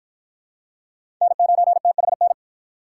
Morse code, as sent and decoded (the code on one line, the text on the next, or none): -. ----. - .... -.
N9THN